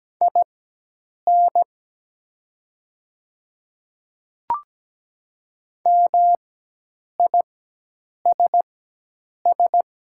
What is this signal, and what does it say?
Morse code, heard as a keyed tone, about 17 wpm, spaced as for 8 wpm, 700 Hz. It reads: IN MISS
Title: 4x Character Spacing - Top 500 Words - 17wpm